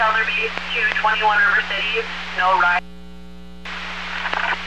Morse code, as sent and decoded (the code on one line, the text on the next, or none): none